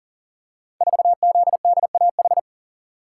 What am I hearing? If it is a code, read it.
Morse code, heard as a keyed tone, about 38 wpm, 690 Hz. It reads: 4ZDAH
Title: Code Groups - 5 Letter:Numbers - 40wpm